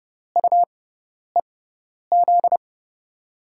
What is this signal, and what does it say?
Morse code, heard as a keyed tone, about 31 wpm, spaced as for 10 wpm, 700 Hz. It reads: UEZ
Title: USA Call Signs - 6 Characters - 6x Character Spacing - 30wpm